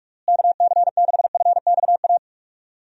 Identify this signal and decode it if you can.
Morse code, heard as a keyed tone, about 47 wpm, 700 Hz. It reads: KC6FXA
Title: USA Call Signs - 6 Characters - Repeated 2x - 45wpm